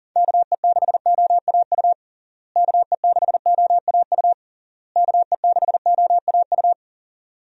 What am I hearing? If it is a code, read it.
Morse code, heard as a keyed tone, about 40 wpm, 700 Hz. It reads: KE6OAU KE6OAU KE6OAU
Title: USA Call Signs - 6 Characters - Repeated 3x - 40wpm